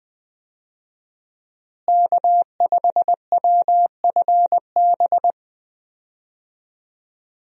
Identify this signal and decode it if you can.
Morse code, heard as a keyed tone, about 20 wpm, 700 Hz. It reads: K5WFB
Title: USA Call Signs - 5 Characters - Repeated 2x - 20wpm